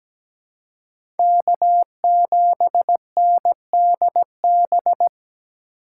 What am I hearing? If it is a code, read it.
Morse code, heard as a keyed tone, about 17 wpm, 700 Hz. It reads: K7NDB
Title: USA Call Signs - 3x Character Spacing - 17wpm